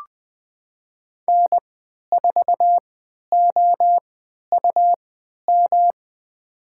Morse code, heard as a keyed tone, about 20 wpm, spaced as for 11 wpm, 700 Hz. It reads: N4OUM
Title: USA Call Signs - 5 Characters - 3x Character Spacing - 20wpm